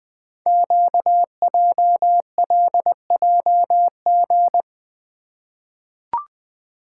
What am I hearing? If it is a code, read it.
Morse code, heard as a keyed tone, about 20 wpm, 700 Hz. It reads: QJLJG
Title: Code Groups - 5 Letters - 20wpm